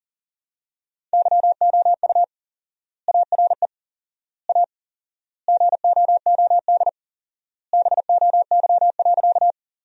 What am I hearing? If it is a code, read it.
Morse code, heard as a keyed tone, about 40 wpm, 700 Hz. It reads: YOU ARE A GOOD BOY.
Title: 4x Word-Spacing - Sentences from Top 200 Words - 40wpm